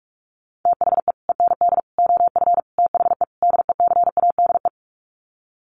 Morse code, heard as a keyed tone, about 44 wpm, 700 Hz. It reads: THE END OF THE DECADE
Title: Sets of 5 Words Encore 45wpm